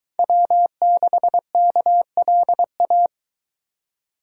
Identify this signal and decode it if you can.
Morse code, heard as a keyed tone, about 24 wpm, 700 Hz. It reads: W6KLA